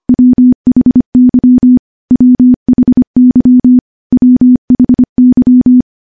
Morse code, heard as a keyed tone, about 25 wpm, 250 Hz. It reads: WHY WHY WHY
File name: morse-training-25WPM